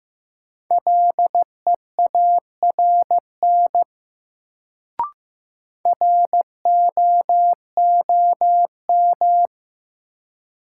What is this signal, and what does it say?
Morse code, heard as a keyed tone, about 15 wpm, 700 Hz. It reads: LEARN ROOM
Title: Sentences from Top 300 Words - New Word Drill - 15wpm